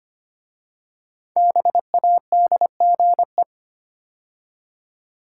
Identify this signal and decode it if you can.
Morse code, heard as a keyed tone, about 25 wpm, 700 Hz. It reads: BADGE
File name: 5 Letter Words - First 3 Letters Given - 25wpm